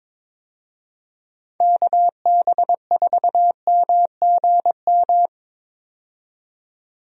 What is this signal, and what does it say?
Morse code, heard as a keyed tone, about 21 wpm, 700 Hz. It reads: KB4MGM
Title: USA Call Signs - 6 Characters - 22wpm